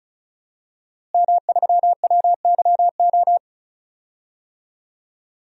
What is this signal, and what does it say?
Morse code, heard as a keyed tone, about 36 wpm, 700 Hz. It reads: M3WYO
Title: DXCC-100 Call Signs - 5 Characters - Repeated 3x - 35wpm